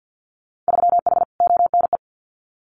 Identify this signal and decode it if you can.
Morse code, heard as a keyed tone, about 50 wpm, 700 Hz. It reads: 3H ONE
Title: Winter Field Day Exchanges Encore 50wpm